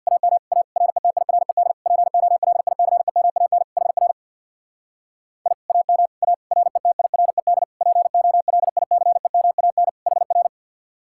Morse code, heard as a keyed tone, about 64 wpm, 700 Hz. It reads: AM A RETIRED POLICEMAN HR I AM A RETIRED POLICEMAN HR